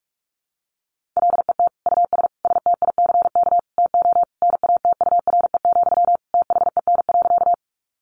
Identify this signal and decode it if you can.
Morse code, heard as a keyed tone, about 44 wpm, 700 Hz. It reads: LET US STICK TO NATURE, THEN.
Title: Sentences from Top 1200 Words 45wpm